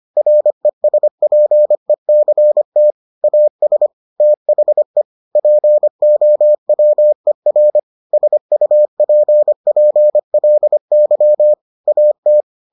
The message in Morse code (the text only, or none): RESPECT AS THE POWER SUPPLY AT